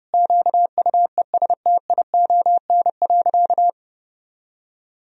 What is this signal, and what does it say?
Morse code, heard as a keyed tone, about 30 wpm, 700 Hz. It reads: QUESTION.
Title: Sentences from Top 1000 Words - Review Part 3 - 30wpm